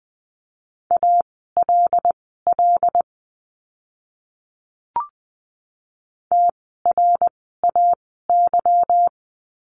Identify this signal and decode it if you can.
Morse code, heard as a keyed tone, about 20 wpm, spaced as for 15 wpm, 700 Hz. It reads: ALL TRAY
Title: 146 Lesson 24f - Letter y - Words 20wpm